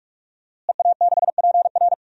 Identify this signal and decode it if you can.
Morse code, heard as a keyed tone, about 46 wpm, 700 Hz. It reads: EA6PR
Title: DXCC-100 Call Signs - 5 Characters - Repeated 3x - 45wpm